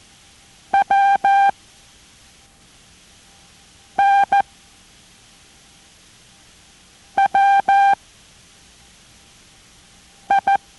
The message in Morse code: WNWI